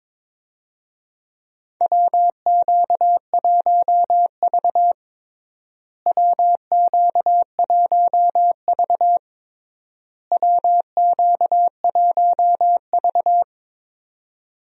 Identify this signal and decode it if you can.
Morse code, heard as a keyed tone, about 22 wpm, 700 Hz. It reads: WQ1V WQ1V WQ1V
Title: USA Call Signs - 4 Characters - Repeated 3x - 22wpm